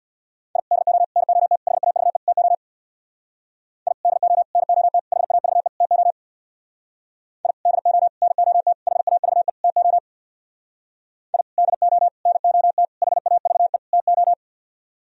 Morse code, heard as a keyed tone, about 61 wpm, 700 Hz. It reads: I DO NOT HAVE TO I DO NOT HAVE TO I DO NOT HAVE TO I DO NOT HAVE TO
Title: Speed-Racing - Sets of 5 Words 50wpm